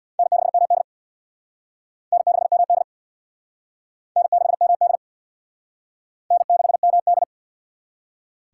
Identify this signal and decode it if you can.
Morse code, heard as a keyed tone, about 65 wpm, 690 Hz. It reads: N6MD N6MD N6MD N6MD